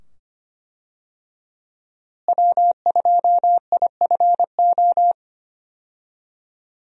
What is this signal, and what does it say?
Morse code, heard as a keyed tone, about 25 wpm, 700 Hz. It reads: W2IFO
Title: USA Call Signs - 5 Characters - Repeated 2x - 25wpm